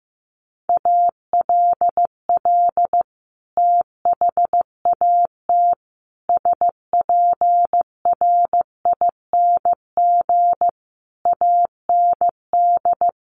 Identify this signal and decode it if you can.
Morse code, heard as a keyed tone, about 15 wpm, 700 Hz. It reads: ALL THAT SPRING AND
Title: Sentences from Top 1200 Words 15wpm